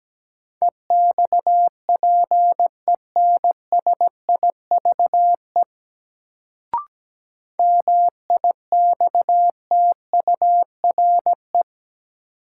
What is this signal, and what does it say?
Morse code, heard as a keyed tone, about 17 wpm, 700 Hz. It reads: EXPENSIVE MIXTURE
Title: Sentences from Top 1500 Words - New Word Drill Alt - 17wpm